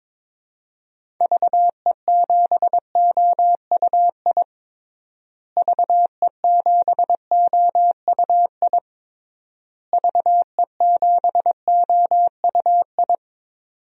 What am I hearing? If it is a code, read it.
Morse code, heard as a keyed tone, about 22 wpm, 700 Hz. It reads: VE7OUI VE7OUI VE7OUI
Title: DXCC-100 Call Signs - 6 Characters - Repeated 3x - 22wpm